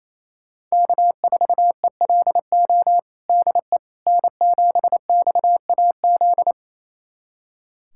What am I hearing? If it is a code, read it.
Morse code, heard as a keyed tone, about 28 wpm, 700 Hz. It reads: K4ELO DE N7XAZ
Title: USA Callsign DE Callsign 28wpm